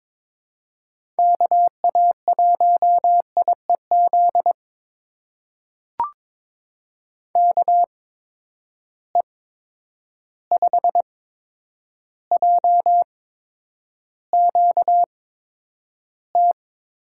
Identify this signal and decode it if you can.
Morse code, heard as a keyed tone, about 22 wpm, spaced as for 9 wpm, 700 Hz. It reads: KA1IEZ KE5JQT
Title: USA Call Signs - 6 Characters - 8x Character Spacing - 22wpm